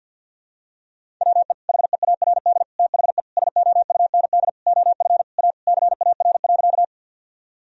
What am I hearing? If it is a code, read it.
Morse code, heard as a keyed tone, about 46 wpm, 700 Hz. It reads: WE HEARD THE SOUND OF A CAR.
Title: Sentences from Top 1000 Words - Review Part 1 - 50wpm